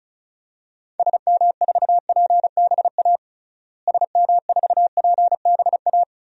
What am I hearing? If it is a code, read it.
Morse code, heard as a keyed tone, about 36 wpm, 700 Hz. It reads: SM4PBA SM4PBA
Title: DXCC-100 Call Signs - 6 Characters - Repeated 3x - 35wpm